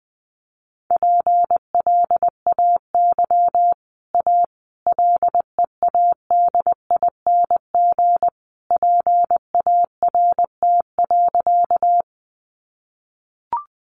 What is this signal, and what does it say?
Morse code, heard as a keyed tone, about 20 wpm, 700 Hz. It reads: PLAY A LEADING PART.
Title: Sentences from Top 1100 Words 20wpm